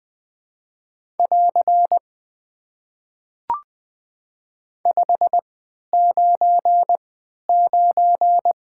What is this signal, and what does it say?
Morse code, heard as a keyed tone, about 20 wpm, spaced as for 11 wpm, 700 Hz. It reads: <AR> 599
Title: Top 90 QSO Elements - 3x Character Spacing - 20wpm